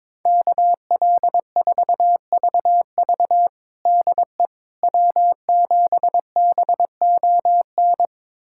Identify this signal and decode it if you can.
Morse code, heard as a keyed tone, about 22 wpm, 700 Hz. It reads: KL4VV DE W7BON
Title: USA Callsign DE Callsign - 5 Characters - 22wpm